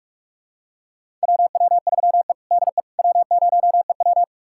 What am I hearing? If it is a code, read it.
Morse code, heard as a keyed tone, about 46 wpm, 700 Hz. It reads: WW3E DE W0EW
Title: USA Callsign DE Callsign - 4 Characters - 45wpm